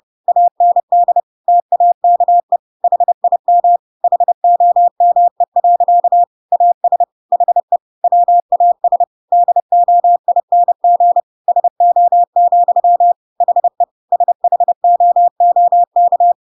Morse code, heard as a keyed tone, about 30 wpm, 700 Hz. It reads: AND TAKE HIM HOME. AS HE WAS DOING SO, HE SHOOK
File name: Pinocchio / Pinocchio ch3 30 30 ik1hge